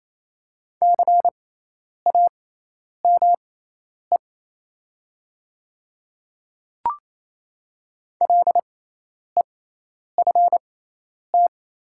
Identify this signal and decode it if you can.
Morse code, heard as a keyed tone, about 28 wpm, spaced as for 10 wpm, 700 Hz. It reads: CAME LEFT
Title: 6x Character Spacing - Top 500 Words - 28wpm